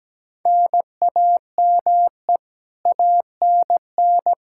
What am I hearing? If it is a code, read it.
Morse code, heard as a keyed tone, about 17 wpm, 700 Hz. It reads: NAME ANN
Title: K1USN SST QSO Practice 17wpm